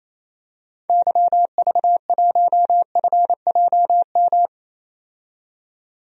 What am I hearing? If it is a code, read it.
Morse code, heard as a keyed tone, about 28 wpm, 700 Hz. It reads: YV1FJM
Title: DXCC-100 Call Signs - 6 Characters - 28wpm